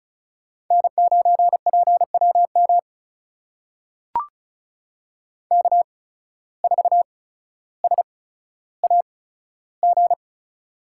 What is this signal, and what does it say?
Morse code, heard as a keyed tone, about 36 wpm, spaced as for 15 wpm, 690 Hz. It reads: N9PWM K4SAG
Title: USA Call Signs - 5 Characters - 8x Character Spacing - 35wpm